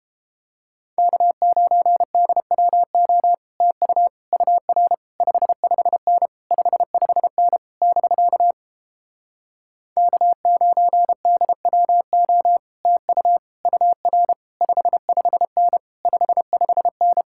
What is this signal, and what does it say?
Morse code, heard as a keyed tone, about 32 wpm, 700 Hz. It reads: K9DWO TU UR 55N 55N <BK> K9DWO TU UR 55N 55N